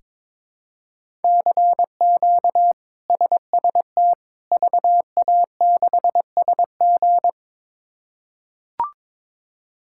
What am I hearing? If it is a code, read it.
Morse code, heard as a keyed tone, about 22 wpm, 700 Hz. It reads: CQ SST VA6SG
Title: K1USN SST QSO Practice - 22wpm